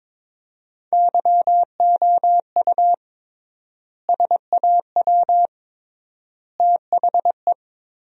Morse code, heard as a keyed tone, about 22 wpm, 700 Hz. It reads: YOU SAW THE
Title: 3x Word-Spacing - Sentences from Top 400 Words - 22wpm